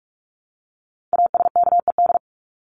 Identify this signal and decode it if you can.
Morse code, heard as a keyed tone, about 44 wpm, 700 Hz. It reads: ASKED
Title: ED Suffix - Top 100 Words with Base Words 45wpm